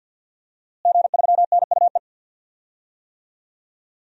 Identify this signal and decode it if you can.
Morse code, heard as a keyed tone, about 52 wpm, 700 Hz. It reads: G3NUE
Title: DXCC-100 Call Signs - 5 Characters - Repeated 3x - 50wpm